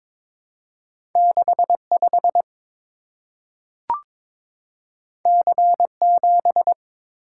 Morse code, heard as a keyed tone, about 22 wpm, 700 Hz. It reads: 65 C7
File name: practice / 22wpm / Two Letter-Number Combinations - 22wpm